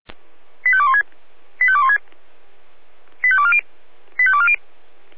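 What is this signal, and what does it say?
Five-tone selective call (ZVEI), heard as quick runs of five tones, tones of about 70 ms. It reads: 85216 85216 85240 85240
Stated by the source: ZVEI five-tone call 88133 received by SDR.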